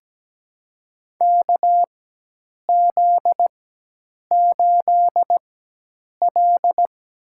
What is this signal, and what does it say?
Morse code, heard as a keyed tone, about 17 wpm, spaced as for 8 wpm, 700 Hz. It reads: KZ8L
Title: USA Call Signs - 4 Characters - 4x Character Spacing - 17wpm